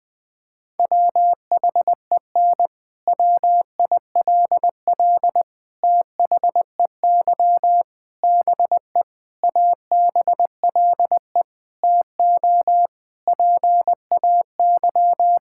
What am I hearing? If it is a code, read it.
Morse code, heard as a keyed tone, about 20 wpm, 700 Hz. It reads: WHEN WILL THEY BE ABLE TO PAY
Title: Sentences from Top 1000 Words - Review Part 5 - 20wpm